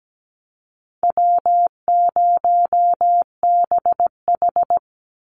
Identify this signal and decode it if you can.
Morse code, heard as a keyed tone, about 17 wpm, 700 Hz. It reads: W0BH
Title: Easy Call Signs Repeated 2x - 17wpm